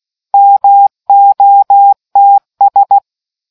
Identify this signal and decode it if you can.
Morse code, heard as a keyed tone, about 16 wpm, 800 Hz. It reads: MOTS